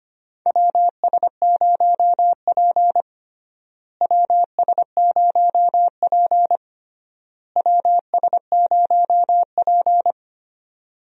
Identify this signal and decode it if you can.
Morse code, heard as a keyed tone, about 25 wpm, 700 Hz. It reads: WS0P WS0P WS0P
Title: USA Call Signs - 4 Characters - Repeated 3x - 25wpm